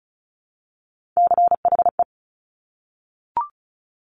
Morse code, heard as a keyed tone, about 36 wpm, 690 Hz. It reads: CHE
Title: Top Three Letter Combinations in Words 35wpm